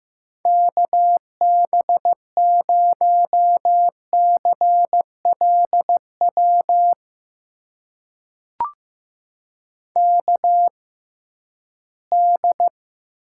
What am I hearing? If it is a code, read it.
Morse code, heard as a keyed tone, about 15 wpm, spaced as for 11 wpm, 700 Hz. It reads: KB0CLW KD
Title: USA Call Signs - 6 Characters - 6x Character Spacing - 15wpm